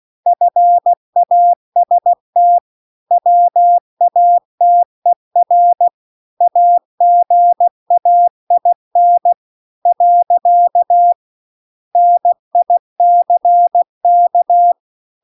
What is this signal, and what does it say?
Morse code, heard as a keyed tone, about 16 wpm, 700 Hz. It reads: FAST WATER AGAIN . NICK
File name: CW-The-Big-Twoheaded-River-1-16WPM